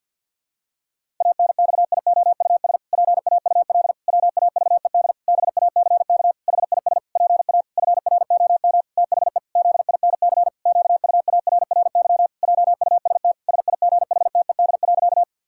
Morse code, heard as a keyed tone, about 51 wpm, 700 Hz. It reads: ANXIOUS PAUL WAVED BACK HIS PA FROM THE ZINC QUARRY JUST SIGHTED.
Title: Pangram Challenge v2 - 50wpm